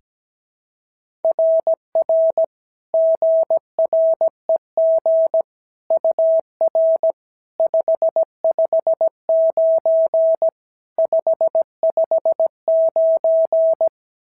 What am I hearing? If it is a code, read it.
Morse code, heard as a keyed tone, about 16 wpm, 650 Hz. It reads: RR GREG UR 559 559
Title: SOTA QSO Practice 17wpm